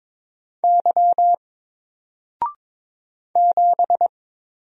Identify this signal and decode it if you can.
Morse code, heard as a keyed tone, about 22 wpm, 700 Hz. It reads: Y 7
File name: Single-Letter-Number-22wpm